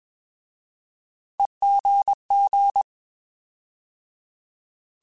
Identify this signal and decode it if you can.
Morse code, heard as a keyed tone, about 21 wpm, 800 Hz. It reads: EGG
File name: Alphabetical / Alphabetical mp3s / Es @20